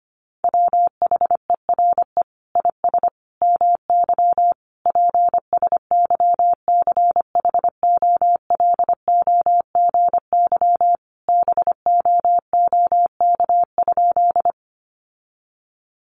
Morse code, heard as a keyed tone, about 25 wpm, 700 Hz. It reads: WHERE IS MY PSYCHOLOGY BOOK?
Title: Sentences from Top 1400 Words 25wpm